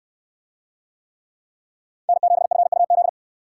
Morse code, heard as a keyed tone, about 64 wpm, 700 Hz. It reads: N7FUZ